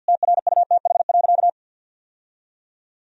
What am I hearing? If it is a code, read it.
Morse code, heard as a keyed tone, about 51 wpm, 700 Hz. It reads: TRUTH.
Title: Sentences from Top 1000 Words - Review Part 1 - 50wpm